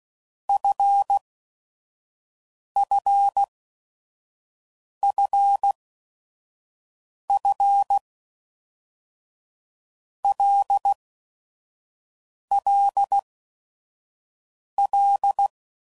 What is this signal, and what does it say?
Morse code, heard as a keyed tone, about 16 wpm, 800 Hz. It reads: F F F F L L L
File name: Basic Letter Groups By7Groups / LFCYQ / LFCYQ@5 f15